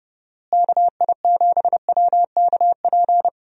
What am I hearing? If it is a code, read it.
Morse code, heard as a keyed tone, about 31 wpm, 700 Hz. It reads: KI7WKP
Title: USA Call Signs - 6 Characters - Repeated 2x - 30wpm